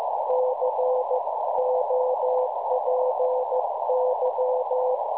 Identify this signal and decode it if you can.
Morse code, heard as a keyed tone, about 15 wpm, 510 Hz, with noise. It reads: COPY
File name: cwronde / websdr recording .6kHz